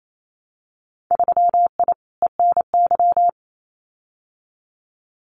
Morse code, heard as a keyed tone, about 28 wpm, 700 Hz. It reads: 3I ENY
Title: Winter Field Day Exchanges Encore 28wpm